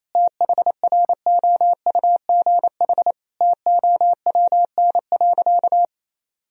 Morse code, heard as a keyed tone, about 27 wpm, 700 Hz. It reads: THROUGH TOWN.
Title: Sentences from Top 1000 Words - Review Part 2 - 28wpm